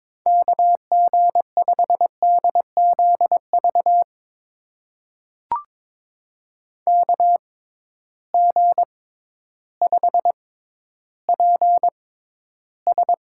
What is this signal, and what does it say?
Morse code, heard as a keyed tone, about 22 wpm, spaced as for 12 wpm, 700 Hz. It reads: KG5DZV KG5PS
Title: USA Call Signs - 6 Characters - 6x Character Spacing - 22wpm